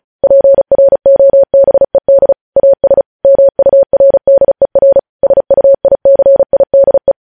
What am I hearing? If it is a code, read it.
Morse code, heard as a keyed tone, about 36 wpm, 550 Hz. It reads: PROBED AS MURDER SUICIDE
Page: News Headlines in Morse Code at 35 WPM